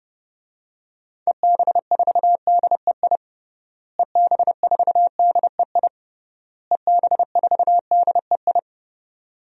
Morse code, heard as a keyed tone, about 32 wpm, 700 Hz. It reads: EB4DEI EB4DEI EB4DEI
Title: DXCC-100 Call Signs - 6 Characters - Repeated 3x - 30wpm